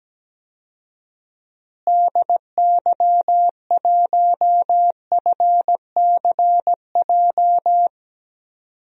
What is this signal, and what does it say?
Morse code, heard as a keyed tone, about 17 wpm, 700 Hz. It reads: DY1FCJ
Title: DXCC-100 Call Signs - 6 Characters - Repeated 3x - 17wpm